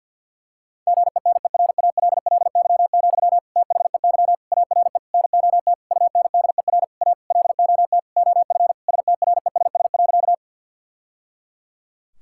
Morse code, heard as a keyed tone, about 50 wpm, 700 Hz. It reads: GENERALLY, THEY ARE NOT UNDER A LOT OF STRESS.
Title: Sentences from Top 1000 Words - Review Part 6 - 50wpm